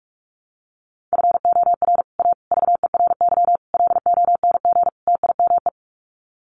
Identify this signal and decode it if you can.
Morse code, heard as a keyed tone, about 44 wpm, 700 Hz. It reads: FOR A VERY LONG TIME
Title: Sets of 5 Words 45wpm